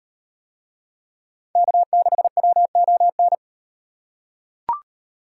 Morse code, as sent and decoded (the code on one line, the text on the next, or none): -.- -.... .-- --- -.
K6WON